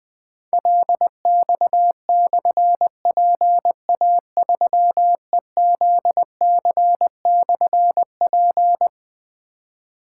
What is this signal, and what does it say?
Morse code, heard as a keyed tone, about 21 wpm, 700 Hz. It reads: LX/PA3EZC/P
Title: Difficult Call Signs 20wpm